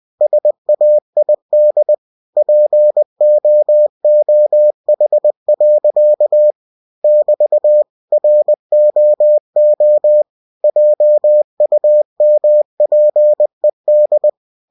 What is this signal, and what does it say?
Morse code, heard as a keyed tone, about 20 wpm, 600 Hz. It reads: SAID POOH. <BT> ROO JUMPED
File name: cw-books / winnie-the-pooh / winnie-20wpm-page-0106